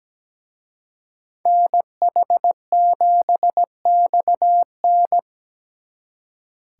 Morse code, heard as a keyed tone, about 20 wpm, 700 Hz. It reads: NH7XN
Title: Speed Racing - USA 2x2 Call Signs 17wpm